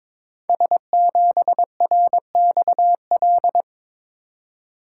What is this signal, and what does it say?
Morse code, heard as a keyed tone, about 22 wpm, 700 Hz. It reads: S7RXL
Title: Code Groups - 5 Letter:Numbers - 22wpm